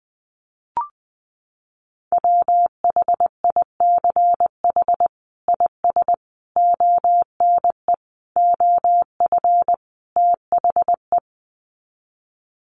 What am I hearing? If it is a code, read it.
Morse code, heard as a keyed tone, about 20 wpm, 700 Hz. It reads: WHICH IS ONE OF THE
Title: Sets of 5 Words Encore 20wpm